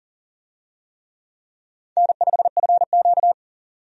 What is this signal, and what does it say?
Morse code, heard as a keyed tone, about 40 wpm, 700 Hz. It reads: N5FQ
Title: USA Call Signs - 4 Characters - 40wpm